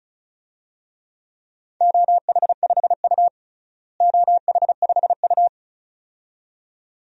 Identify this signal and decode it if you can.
Morse code, heard as a keyed tone, about 34 wpm, 700 Hz. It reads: OH5U OH5U
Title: DXCC-100 Call Signs - 4 Characters - Repeated 2x - 35wpm